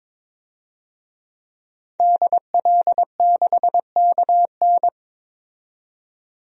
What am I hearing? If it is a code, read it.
Morse code, heard as a keyed tone, about 22 wpm, 700 Hz. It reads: DL6KN
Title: DXCC-100 Call Signs - 5 Characters - 22wpm